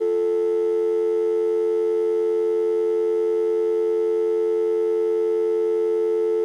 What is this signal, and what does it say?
Dial tone (North America), 350 + 440 Hz.
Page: dial-tone